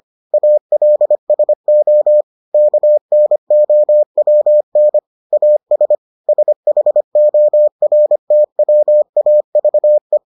Morse code, heard as a keyed tone, about 25 wpm, 600 Hz. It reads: ALSO KNOWN AS SHORTWAVE